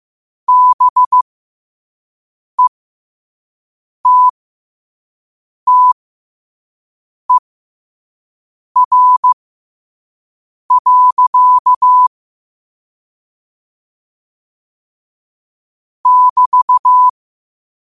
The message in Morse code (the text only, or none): BETTER. <BT>